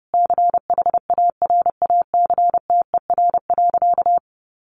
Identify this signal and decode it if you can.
Morse code, heard as a keyed tone, about 28 wpm, 700 Hz. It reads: CHARACTER.
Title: Sentences from Top 1200 Words 30wpm